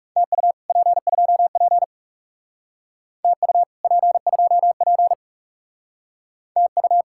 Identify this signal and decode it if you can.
Morse code, heard as a keyed tone, about 35 wpm, 700 Hz. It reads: TU P2P TU P2P TU
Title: Speed-Racing Common QSO Phrases - 30wpm